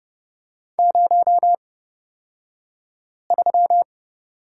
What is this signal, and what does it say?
Morse code, heard as a keyed tone, about 31 wpm, spaced as for 5 wpm, 700 Hz. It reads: 03
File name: Lesson 36d - BT - Two Characters 30wpm